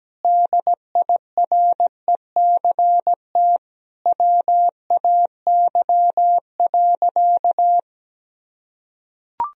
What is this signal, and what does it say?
Morse code, heard as a keyed tone, about 17 wpm, 700 Hz. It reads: DIRECT WAY.
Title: Sentences from Top 1000 Words - Review Part 2 - 17wpm